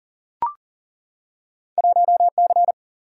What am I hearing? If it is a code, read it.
Morse code, heard as a keyed tone, about 40 wpm, 700 Hz. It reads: 1C
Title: Single-Letter Single-Number 40wpm